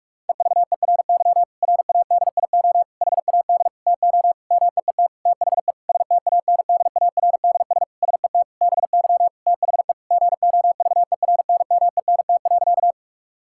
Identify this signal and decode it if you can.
Morse code, heard as a keyed tone, about 44 wpm, 700 Hz. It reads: EVERY RADIO HAD TO MEET THE STANDARDS SET BY THE GOVERNMENT.
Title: Sentences from Top 1000 Words - Review Part 6 - 45wpm